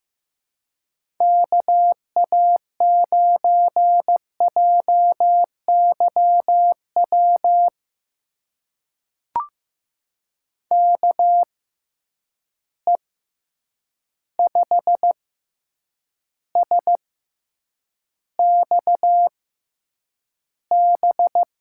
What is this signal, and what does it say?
Morse code, heard as a keyed tone, about 15 wpm, spaced as for 8 wpm, 700 Hz. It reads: KA9JYW KE5SXB
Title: USA Call Signs - 6 Characters - 6x Character Spacing - 15wpm